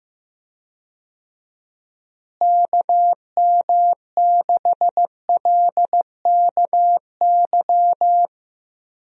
Morse code, heard as a keyed tone, about 15 wpm, 700 Hz. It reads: KM6LKY
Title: USA Call Signs - 6 Characters - Repeated 2x - 15wpm